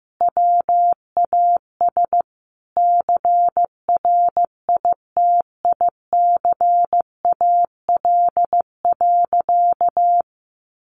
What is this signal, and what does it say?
Morse code, heard as a keyed tone, about 15 wpm, 700 Hz. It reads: WAS CRITICAL.
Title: Sentences from Top 1100 Words 15wpm